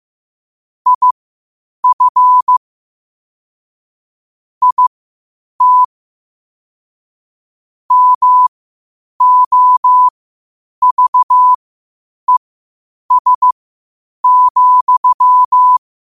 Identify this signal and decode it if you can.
Morse code, heard as a keyed tone, about 15 wpm, spaced as for 8 wpm, 1000 Hz. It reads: IF IT MOVES,
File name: qotdfiles / 8wpm